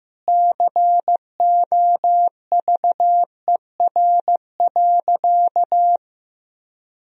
Morse code, heard as a keyed tone, about 15 wpm, 700 Hz. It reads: COVER.
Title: Sentences from Top 1000 Words - Review Part 1 - 15wpm